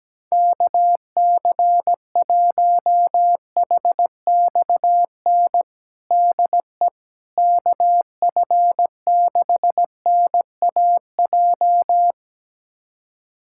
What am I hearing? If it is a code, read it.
Morse code, heard as a keyed tone, about 17 wpm, 700 Hz. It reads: KC1HXN DE KF6NAJ
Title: USA Callsign DE Callsign - 6 Characters - 17wpm